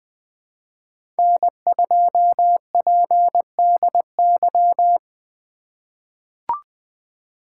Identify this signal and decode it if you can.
Morse code, heard as a keyed tone, about 20 wpm, 700 Hz. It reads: N2PDY
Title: USA Call Signs - 5 Characters - 3x Character Spacing - 20wpm